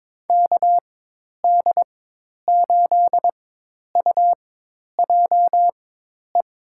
Morse code, heard as a keyed tone, about 22 wpm, spaced as for 10 wpm, 700 Hz. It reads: KD8UJE
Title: USA Call Signs - 6 Characters - 4x Character Spacing - 22wpm